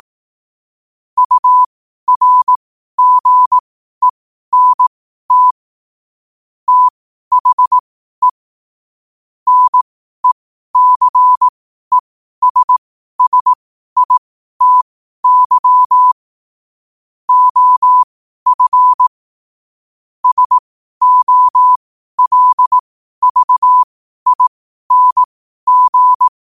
Morse code, heard as a keyed tone, about 18 wpm, spaced as for 12 wpm, 1000 Hz. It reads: URGENT THE NECESSITY OF SOLVING
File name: qotdfiles / 12wpm